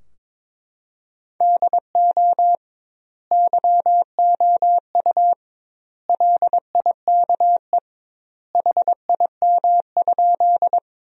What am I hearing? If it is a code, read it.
Morse code, heard as a keyed tone, about 22 wpm, 700 Hz. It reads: DO YOU LIKE HIM?
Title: 2x Word-Spacing - Sentences from Top 100 Words - 22wpm